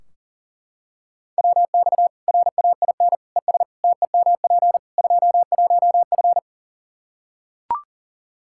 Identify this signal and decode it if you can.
Morse code, heard as a keyed tone, about 40 wpm, 700 Hz. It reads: WX RAIN ES TEMP 21F
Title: Practice QSO - Exchange 2 - Info - 40wpm